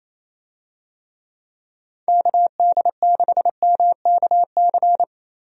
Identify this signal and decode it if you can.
Morse code, heard as a keyed tone, about 28 wpm, 700 Hz. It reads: KD6MKC
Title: USA Call Signs - 6 Characters - 28wpm